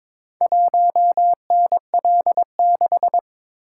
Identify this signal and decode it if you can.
Morse code, heard as a keyed tone, about 22 wpm, 700 Hz. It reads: 1NL6
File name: practice / 22wpm / Four Letter-Number Combinations - 22wpm